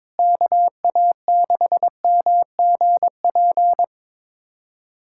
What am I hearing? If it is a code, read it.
Morse code, heard as a keyed tone, about 22 wpm, 700 Hz. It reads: KA6MGP
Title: USA Callsign DE Callsign - 6 Characters - 22wpm